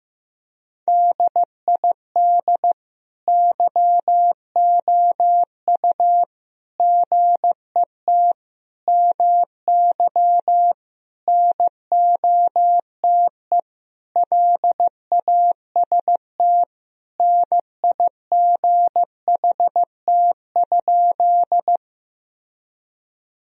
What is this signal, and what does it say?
Morse code, heard as a keyed tone, about 15 wpm, 700 Hz. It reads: DID YOU GET MY NOTE LAST NIGHT?
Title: Sentences from Top 1000 Words - Review Part 2 - 15wpm